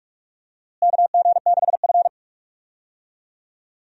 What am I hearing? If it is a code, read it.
Morse code, heard as a keyed tone, about 43 wpm, 700 Hz. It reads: KG6F
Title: USA Call Signs - 4 Characters - Repeated 2x - 45wpm